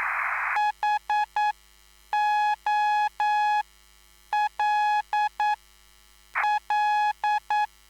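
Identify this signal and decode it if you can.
Morse code, heard as a keyed tone, about 9 wpm, spaced as for 7 wpm, 840 Hz. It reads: HOLL
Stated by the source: Live CW QSO between PA2BX and SP3JFK on the 14055 kHz QRS frequency, at a basic 8 wpm.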